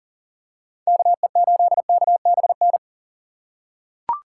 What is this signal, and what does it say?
Morse code, heard as a keyed tone, about 40 wpm, 690 Hz. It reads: KE8KBN
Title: USA Call Signs - 4x Character Spacing - 40wpm